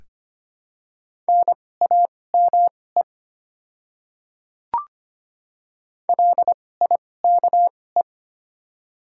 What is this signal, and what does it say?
Morse code, heard as a keyed tone, about 25 wpm, spaced as for 18 wpm, 700 Hz. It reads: NAME LIKE
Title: 2x Character Spacing - Top 300 Words - 25wpm